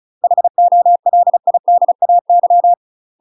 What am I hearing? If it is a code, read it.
Morse code, heard as a keyed tone, about 34 wpm, 700 Hz. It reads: HOLIDAY
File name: UK MASTER QSOs TRAINING / 35WPM / UK MASTER QSOs 0005